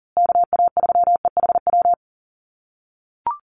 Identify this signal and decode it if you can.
Morse code, heard as a keyed tone, about 40 wpm, 700 Hz. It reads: KA3EHW
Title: USA Call Signs Repeated 3x 40wpm